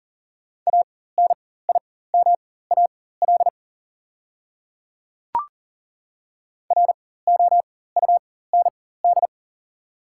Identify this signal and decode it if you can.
Morse code, heard as a keyed tone, about 40 wpm, spaced as for 19 wpm, 690 Hz. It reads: ANIMAL ROUND
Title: 4x Character Spacing - Top 200 Words - 40wpm